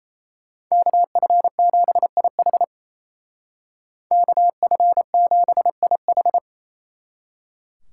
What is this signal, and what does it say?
Morse code, heard as a keyed tone, about 36 wpm, 700 Hz. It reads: KF7IH KF7IH